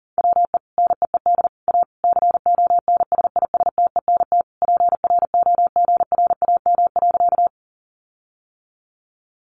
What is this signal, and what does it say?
Morse code, heard as a keyed tone, about 40 wpm, 700 Hz. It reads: WE NEED A CONSISTENT PROGRAM.
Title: Sentences from Top 1500 Words 40wpm